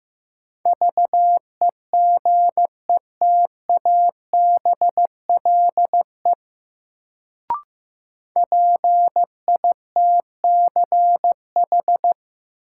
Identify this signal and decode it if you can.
Morse code, heard as a keyed tone, about 15 wpm, 700 Hz. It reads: VEGETABLE PITCH
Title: Sentences from Top 1300 Words - New Word Drill - 15wpm